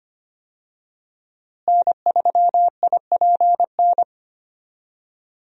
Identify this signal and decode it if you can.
Morse code, heard as a keyed tone, about 25 wpm, 700 Hz. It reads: N3IPN
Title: USA Call Signs - 5 Characters - Repeated 3x - 25wpm